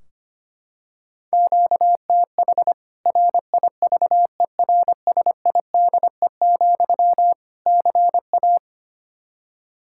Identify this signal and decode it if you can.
Morse code, heard as a keyed tone, about 25 wpm, 700 Hz. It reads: QTH RIVERSIDE, CA
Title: K1USN SST QSO Practice 25wpm